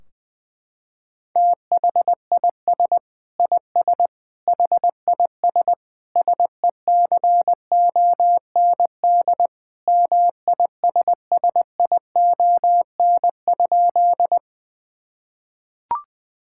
Morse code, heard as a keyed tone, about 20 wpm, 700 Hz. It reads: THIS IS HIS SECOND MISSION?
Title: Sentences from Top 1600 Words 20wpm